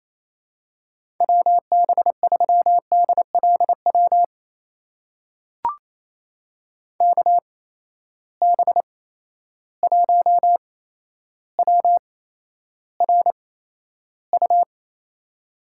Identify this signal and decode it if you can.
Morse code, heard as a keyed tone, about 28 wpm, spaced as for 12 wpm, 700 Hz. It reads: WB3DLW KB1WRU
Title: USA Call Signs - 6 Characters - 8x Character Spacing - 28wpm